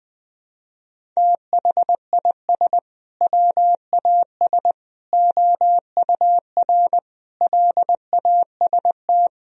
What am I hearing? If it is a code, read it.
Morse code, heard as a keyed tone, about 20 wpm, 700 Hz. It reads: THIS WAS OUR LAST
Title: Sentences from Top 1000 Words - Review Part 1 - 20wpm